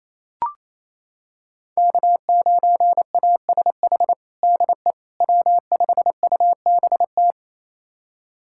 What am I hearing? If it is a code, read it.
Morse code, heard as a keyed tone, about 28 wpm, 700 Hz. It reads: K9ASH DE W5UBT
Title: USA Callsign DE Callsign - 5 Characters - 28wpm